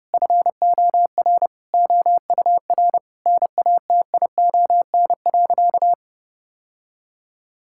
Morse code, heard as a keyed tone, about 30 wpm, 700 Hz. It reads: FOR OUR NATION.
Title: Sentences from Top 1000 Words - Review Part 6 - 30wpm